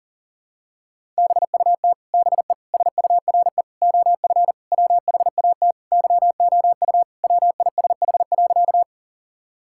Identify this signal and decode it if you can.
Morse code, heard as a keyed tone, about 40 wpm, 700 Hz. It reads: BUT BE SURE OF WHAT YOU WISH.
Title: Sentences from Top 1600 Words 40wpm